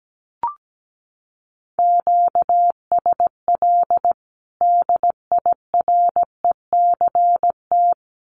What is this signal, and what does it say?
Morse code, heard as a keyed tone, about 17 wpm, 700 Hz. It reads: QSL DIRECT
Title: Common QSO Phrases 17wpm